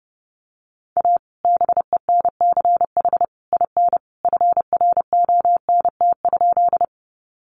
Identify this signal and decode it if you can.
Morse code, heard as a keyed tone, about 30 wpm, 700 Hz. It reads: A BENCH IN FRONT?
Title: Sentences from Top 1400 Words 30wpm